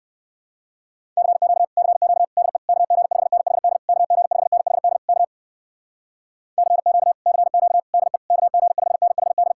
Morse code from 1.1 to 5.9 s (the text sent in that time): CQ CQ DE KG5NHD KG5NHD K